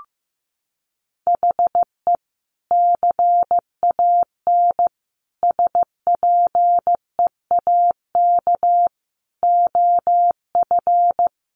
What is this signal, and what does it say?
Morse code, heard as a keyed tone, about 15 wpm, 700 Hz. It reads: HE CAN SPEAK OF
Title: Sentences from Top 1100 Words 15wpm